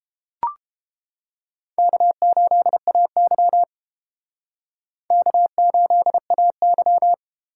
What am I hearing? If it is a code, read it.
Morse code, heard as a keyed tone, about 31 wpm, 700 Hz. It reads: K8AY K8AY